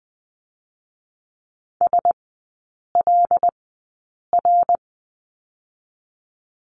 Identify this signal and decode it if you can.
Morse code, heard as a keyed tone, about 20 wpm, spaced as for 8 wpm, 700 Hz. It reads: SLR
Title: 081 Lesson 14e - Number 2 - Three Characters 20wpm